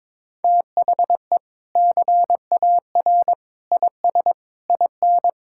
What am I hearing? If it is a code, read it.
Morse code, heard as a keyed tone, about 22 wpm, 700 Hz. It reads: THE CAR IS IN
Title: Sentences from Top 1600 Words 22wpm